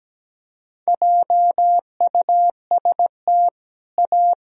Morse code, heard as a keyed tone, about 17 wpm, 700 Hz. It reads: JUST A
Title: Sentences from Top 1000 Words - Review Part 1 - 17wpm